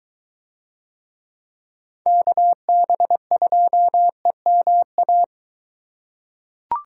Morse code, heard as a keyed tone, about 22 wpm, 700 Hz. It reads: KB2EMA